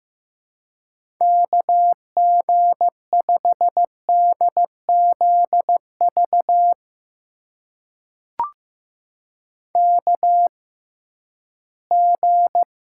Morse code, heard as a keyed tone, about 15 wpm, spaced as for 11 wpm, 700 Hz. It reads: KG5DZV KG
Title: USA Call Signs - 6 Characters - 6x Character Spacing - 15wpm